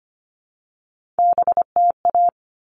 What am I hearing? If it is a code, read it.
Morse code, heard as a keyed tone, about 25 wpm, 700 Hz. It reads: BTA
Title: All Cut Numbers - 3-Digit 25wpm